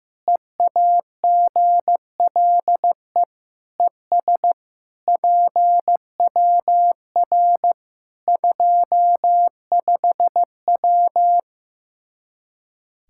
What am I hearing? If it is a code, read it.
Morse code, heard as a keyed tone, about 15 wpm, 700 Hz. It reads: EAGLE ES PWR 25W
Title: Practice QSO - Exchange 2 - Info - 15wpm